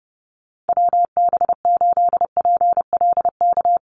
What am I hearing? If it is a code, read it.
Morse code, heard as a keyed tone, about 30 wpm, 700 Hz. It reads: WB8PLK
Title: USA Call Signs Repeated 2x 30wpm